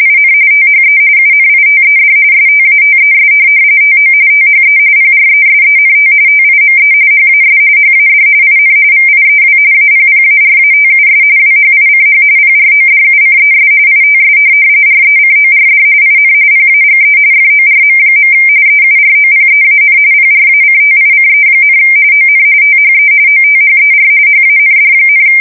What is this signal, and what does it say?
RTTY, about 45.45 baud. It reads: IN THE UNITED STATES TO THE PRESENT TIME HAS HAD IN VIEW APPARATUS WHICH PRODUCES ITS RECORD IN PAGE FORM. EUROPEAN PRACTICE, ON THE OTHER HAND,
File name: fsk170 / printing-telegraph